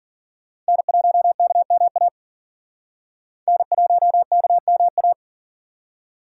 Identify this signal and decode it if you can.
Morse code, heard as a keyed tone, about 49 wpm, 700 Hz. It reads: N1KMA N1KMA
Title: Speed Racing - USA Call Signs 40wpm